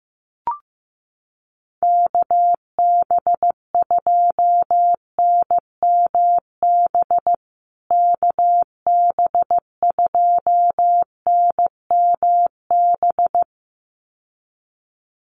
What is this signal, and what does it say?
Morse code, heard as a keyed tone, about 15 wpm, 700 Hz. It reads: KB2NMB KB2NMB
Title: USA Call Signs Repeated 2x 15wpm